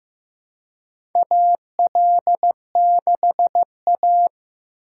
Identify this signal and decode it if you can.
Morse code, heard as a keyed tone, about 15 wpm, 700 Hz. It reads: AL6A
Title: USA Call Signs - 4 Characters - 3x Character Spacing - 15wpm